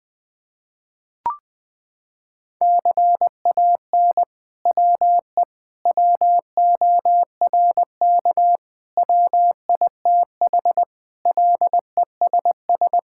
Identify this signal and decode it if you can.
Morse code, heard as a keyed tone, about 20 wpm, 700 Hz. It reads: CAN WE WORK WITH LESS
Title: Sentences from Top 1000 Words - Review Part 5 - 20wpm